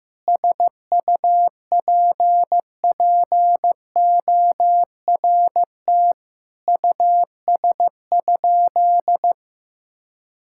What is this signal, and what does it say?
Morse code, heard as a keyed tone, about 15 wpm, 700 Hz. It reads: SUPPORT US?
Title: Sentences from Top 1000 Words - Review Part 5 - 15wpm